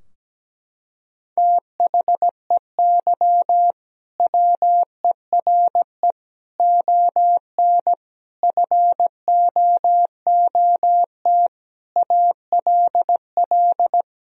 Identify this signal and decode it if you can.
Morse code, heard as a keyed tone, about 17 wpm, 700 Hz. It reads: THEY WERE ON FOOT ALL
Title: Sentences from Top 1000 Words - Review Part 3 - 17wpm